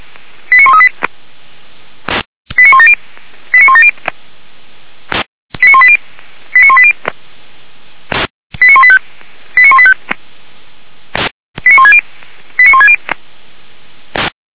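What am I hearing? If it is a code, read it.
Five-tone selective call (ZVEI), heard as quick runs of five tones, tones of about 70 ms. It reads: 89138 89170 89170 89180 89180 89185 89185 89160 89160
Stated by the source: ZVEI five-tone alert picked up by SDR, from Feuerwehr Orlamünde.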